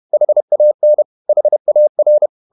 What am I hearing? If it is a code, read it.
Morse code, heard as a keyed tone, about 31 wpm, 600 Hz. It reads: HAN HAR
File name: Ferie QRQ 31wpm